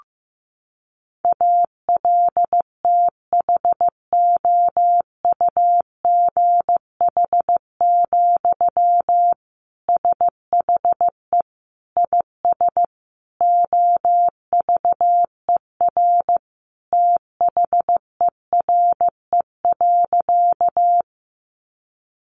Morse code, heard as a keyed tone, about 15 wpm, 700 Hz. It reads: ALTHOUGH, SHE IS OVER THERE.
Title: Sentences from Top 600 Words 15wpm